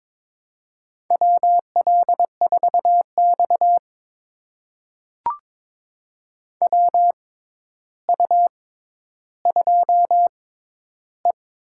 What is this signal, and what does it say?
Morse code, heard as a keyed tone, about 22 wpm, spaced as for 11 wpm, 700 Hz. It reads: WL4X WU2E
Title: USA Call Signs - 4 Characters - 6x Character Spacing - 22wpm